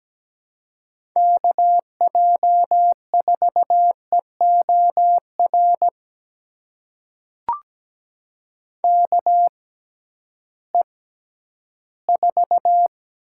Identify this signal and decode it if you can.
Morse code, heard as a keyed tone, about 17 wpm, spaced as for 11 wpm, 700 Hz. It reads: KJ4EOR KE4
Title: USA Call Signs - 6 Characters - 6x Character Spacing - 17wpm